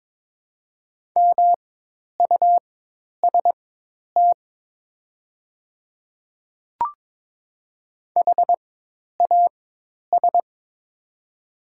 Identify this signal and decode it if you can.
Morse code, heard as a keyed tone, about 22 wpm, spaced as for 10 wpm, 700 Hz. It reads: MUST HAS
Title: 4x Character Spacing - Top 500 Words - 22wpm